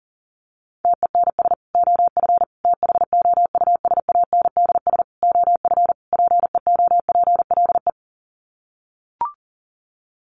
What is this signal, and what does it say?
Morse code, heard as a keyed tone, about 37 wpm, 700 Hz. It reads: TENS OF THOUSANDS OF PEOPLE
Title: Sets of 5 Words Encore2 40wpm